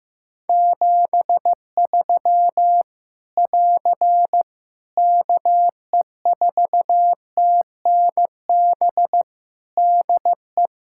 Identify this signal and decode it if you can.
Morse code, heard as a keyed tone, about 15 wpm, 700 Hz. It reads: 73 <AR> KE4TNB DE
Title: K1USN SST QSO Practice 15wpm